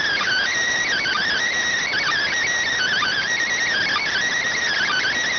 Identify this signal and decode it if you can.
Emergency Alert System (SAME) tones, heard as harsh buzzing bursts. It is heard from 1.0 to 5.1 s.